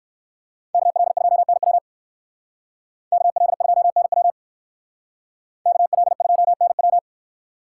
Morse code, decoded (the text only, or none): KL2NW KL2NW KL2NW